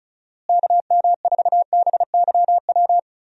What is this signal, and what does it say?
Morse code, heard as a keyed tone, about 34 wpm, 700 Hz. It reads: KM4BYW
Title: USA Call Signs - 6 Characters - Repeated 3x - 35wpm